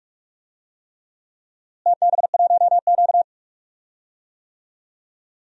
Morse code, heard as a keyed tone, about 41 wpm, 700 Hz. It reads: TB1Q
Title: DXCC-100 Call Signs - 4 Characters - Repeated 3x - 45wpm